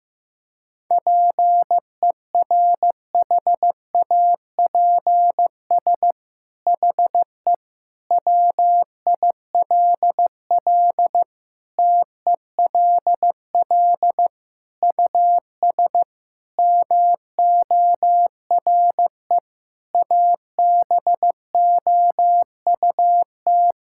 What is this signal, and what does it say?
Morse code, heard as a keyed tone, about 15 wpm, 700 Hz. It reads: PERHAPS HE WILL TELL US MORE ABOUT
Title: Sentences from Top 1600 Words 15wpm